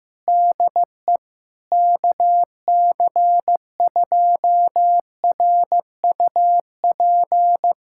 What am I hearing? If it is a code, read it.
Morse code, heard as a keyed tone, about 16 wpm, 700 Hz. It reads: DE KC2RUP
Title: USA Callsign DE Callsign - 6 Characters - 15wpm